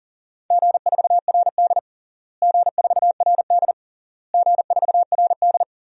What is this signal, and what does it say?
Morse code, heard as a keyed tone, about 40 wpm, 700 Hz. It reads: G4RD G4RD G4RD
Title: DXCC-100 Call Signs - 4 Characters - Repeated 3x - 40wpm